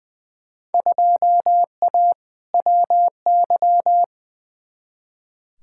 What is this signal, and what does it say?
Morse code, heard as a keyed tone, about 20 wpm, 700 Hz. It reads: 2A WY
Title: Field Day Exchanges - 20wpm